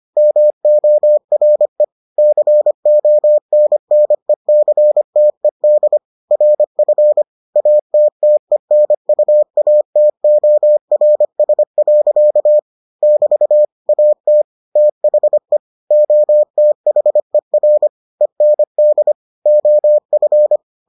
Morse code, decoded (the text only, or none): MORE CONNECTED RF ATTENUATORS. <BT> AT THE OTHER END OF